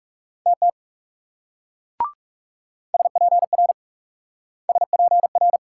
Morse code, heard as a keyed tone, about 37 wpm, 700 Hz. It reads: TT SPR SPR
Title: Speed-Racing Multi-letter Phonemes - 30wpm